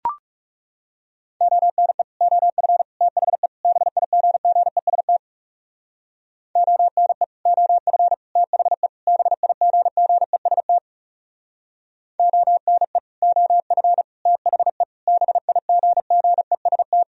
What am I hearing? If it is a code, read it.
Morse code, heard as a keyed tone, about 36 wpm, 700 Hz. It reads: ONE OF THE BIGGEST ONE OF THE BIGGEST ONE OF THE BIGGEST
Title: Speed-Racing - Sets of 4 Words 30wpm